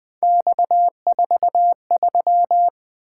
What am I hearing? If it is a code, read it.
Morse code, heard as a keyed tone, about 20 wpm, 700 Hz. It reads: X43
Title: Three Letter-Number Combinations - 20wpm